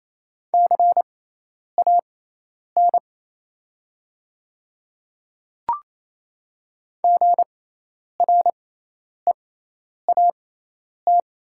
Morse code, heard as a keyed tone, about 28 wpm, spaced as for 10 wpm, 700 Hz. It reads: CAN GREAT
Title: 6x Character Spacing - Top 200 Words - 28wpm